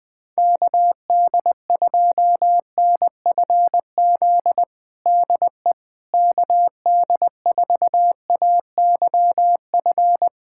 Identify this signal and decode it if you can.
Morse code, heard as a keyed tone, about 20 wpm, 700 Hz. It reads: KD2NFZ DE KD4AYF
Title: USA Callsign DE Callsign - 6 Characters - 20wpm